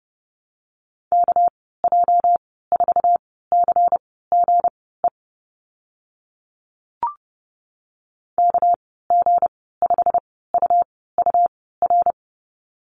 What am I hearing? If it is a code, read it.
Morse code, heard as a keyed tone, about 30 wpm, spaced as for 17 wpm, 700 Hz. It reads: KJ4CGE KG5UUR
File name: Lesson 37g - Letter x - Call Signs 30wpm